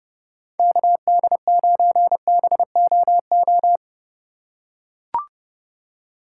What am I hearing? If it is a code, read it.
Morse code, heard as a keyed tone, about 30 wpm, 700 Hz. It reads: KD9BOO
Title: USA Call Signs - 6 Characters - 4x Character Spacing - 30wpm